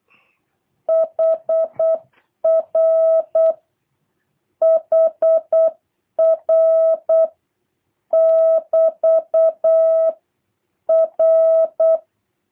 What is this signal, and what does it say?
Morse code, heard as a keyed tone, about 8 wpm, 650 Hz. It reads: HR HR <BT> R
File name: jh8cbh / 2023cw / review1-2